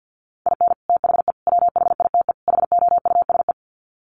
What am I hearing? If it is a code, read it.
Morse code, heard as a keyed tone, about 50 wpm, 700 Hz. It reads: IN THE WHITE HOUSE
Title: Sets of 4 Words 50wpm